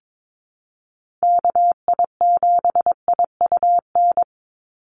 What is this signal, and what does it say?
Morse code, heard as a keyed tone, about 22 wpm, 700 Hz. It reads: KI7IUN
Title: USA Call Signs 22wpm